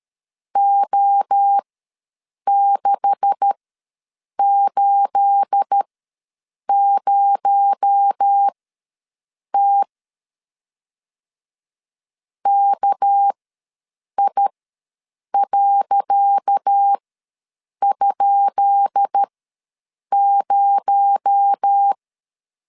Morse code, heard as a keyed tone, about 12 wpm, spaced as for 7 wpm, 800 Hz. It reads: O680T KI.?0
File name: LEZIONI / 10 SIGNS / 10C DOT ws 12wpm 3548